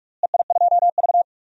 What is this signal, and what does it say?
Morse code, heard as a keyed tone, about 41 wpm, 700 Hz. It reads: EI2V
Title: DXCC-100 Call Signs - 4 Characters - Repeated 2x - 45wpm